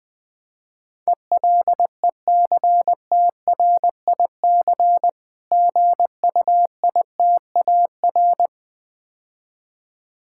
Morse code, heard as a keyed tone, about 20 wpm, 700 Hz. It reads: ELECTRIC GUITAR
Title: Difficult Words - Speed-Racing with Hints - 20wpm